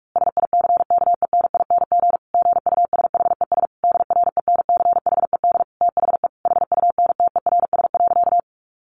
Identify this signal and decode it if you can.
Morse code, heard as a keyed tone, about 44 wpm, 700 Hz. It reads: SICKENING GUSHES DRENCHED THE HUNTERS.
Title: A Sound Of Thunder 45wpm